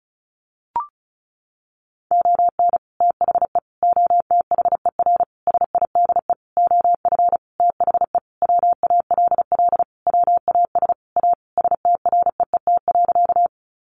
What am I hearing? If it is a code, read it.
Morse code, heard as a keyed tone, about 34 wpm, 700 Hz. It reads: ON THE OTHER SIDE OF THE WALL WAS A STREET.
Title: Sentences from Top 1100 Words 35wpm